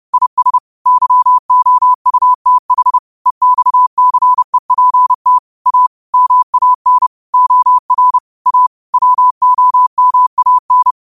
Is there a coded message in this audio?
Yes: Morse code from 0.1 to 10.9 s, heard as a keyed tone, about 30 wpm, 1000 Hz. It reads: IS YOUTH EXCEPT A MAN OR A WOMAN